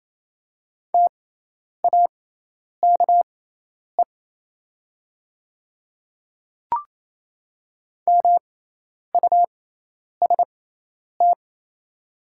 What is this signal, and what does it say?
Morse code, heard as a keyed tone, about 28 wpm, spaced as for 10 wpm, 700 Hz. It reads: TAKE MUST
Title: 6x Character Spacing - Top 200 Words - 28wpm